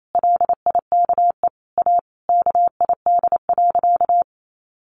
Morse code, heard as a keyed tone, about 28 wpm, 700 Hz. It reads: LIKE A KID.
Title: A Sound Of Thunder 28wpm